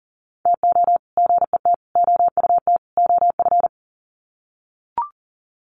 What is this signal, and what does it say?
Morse code, heard as a keyed tone, about 37 wpm, 700 Hz. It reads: TO GET OUT OF
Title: Sets of 4 Words 40wpm